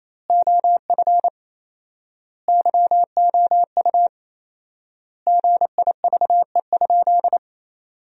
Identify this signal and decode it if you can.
Morse code, heard as a keyed tone, about 28 wpm, 700 Hz. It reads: OF YOU GIVE?
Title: 4x Word-Spacing - Sentences from Top 200 Words - 28wpm